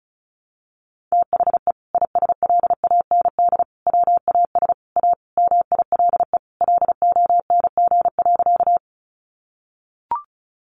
Morse code, heard as a keyed tone, about 34 wpm, 700 Hz. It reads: THE ISLAND WAS A MILE LONG.
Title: Sentences from Top 500 Words 35wpm